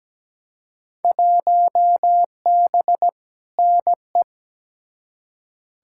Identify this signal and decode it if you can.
Morse code, heard as a keyed tone, about 17 wpm, 700 Hz. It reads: 1B NE
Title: Field Day Exchanges Encore - 17wpm